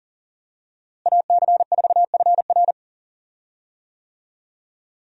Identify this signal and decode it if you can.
Morse code, heard as a keyed tone, about 40 wpm, 700 Hz. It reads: AC4FR
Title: USA Call Signs - 5 Characters - Repeated 2x - 40wpm